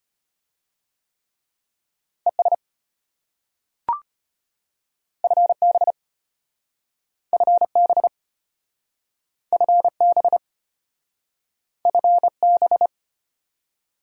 Morse code, heard as a keyed tone, about 32 wpm, 700 Hz. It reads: ES FB FB FB FB